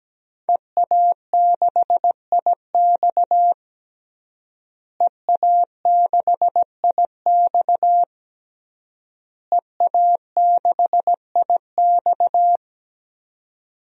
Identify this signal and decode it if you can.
Morse code, heard as a keyed tone, about 17 wpm, 700 Hz. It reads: EA6IX EA6IX EA6IX
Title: DXCC-100 Call Signs - 5 Characters - Repeated 3x - 17wpm